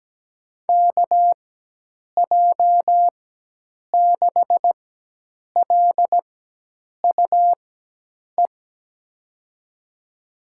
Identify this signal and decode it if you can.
Morse code, heard as a keyed tone, about 17 wpm, spaced as for 8 wpm, 700 Hz. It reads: KJ6LUE
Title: USA Call Signs - 6 Characters - 4x Character Spacing - 17wpm